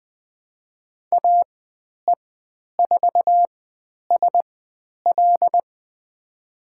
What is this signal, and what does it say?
Morse code, heard as a keyed tone, about 20 wpm, spaced as for 10 wpm, 700 Hz. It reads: AE4SL
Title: 109 Lesson 18g - Letter m - Call Signs 20wpm